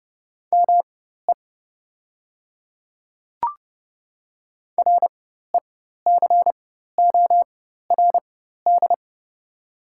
Morse code, heard as a keyed tone, about 32 wpm, spaced as for 14 wpm, 690 Hz. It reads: ME RECORD
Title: 4x Character Spacing - Top 500 Words - 30wpm